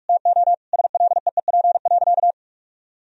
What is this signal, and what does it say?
Morse code, heard as a keyed tone, about 46 wpm, 700 Hz. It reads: TO SLEEP.
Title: Sentences from Top 1000 Words - Review Part 2 - 45wpm